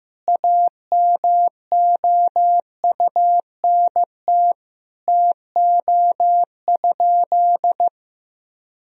Morse code, heard as a keyed tone, about 15 wpm, 700 Hz. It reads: AMOUNT TO?
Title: Sentences from Top 1000 Words - Review Part 4 - 15wpm